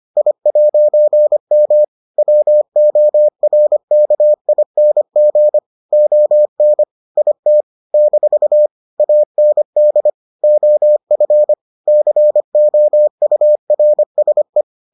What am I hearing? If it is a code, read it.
Morse code, heard as a keyed tone, about 25 wpm, 600 Hz. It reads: I'M WORKING ON IT - AND OF COURSE